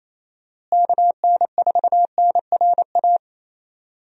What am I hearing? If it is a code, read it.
Morse code, heard as a keyed tone, about 28 wpm, 700 Hz. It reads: KN4NRA
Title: USA Call Signs - 6 Characters - Repeated 3x - 28wpm